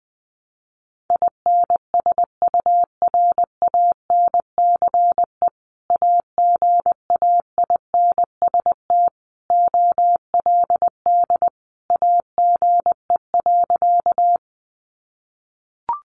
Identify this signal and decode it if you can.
Morse code, heard as a keyed tone, about 20 wpm, 700 Hz. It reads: INSURANCE AGAINST OLD AGE.
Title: Sentences from Top 1100 Words 20wpm